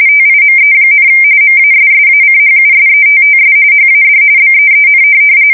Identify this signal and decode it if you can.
RTTY, about 45.45 baud. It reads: TIME, MASSACHUSETTS HAD BEEN
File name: fsk170 / USS-Massachusetts